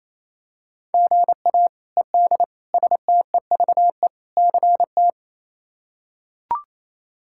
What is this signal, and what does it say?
Morse code, heard as a keyed tone, about 28 wpm, 700 Hz. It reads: GA ED STEVE CT
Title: K1USN SST QSO Practice - 28wpm